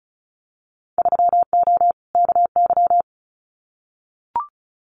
Morse code, heard as a keyed tone, about 36 wpm, 700 Hz. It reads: 3O KY
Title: Winter Field Day Exchanges 35wpm